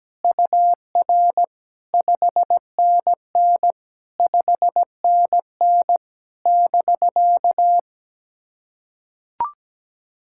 Morse code, heard as a keyed tone, about 17 wpm, 700 Hz. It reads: UR 5NN 5NN <BK>
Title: SOTA QSO Practice 17wpm